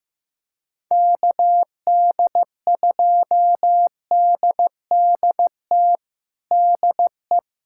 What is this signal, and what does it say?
Morse code, heard as a keyed tone, about 15 wpm, 700 Hz. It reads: KD2DDT DE
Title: USA Callsign DE Callsign - 6 Characters - 15wpm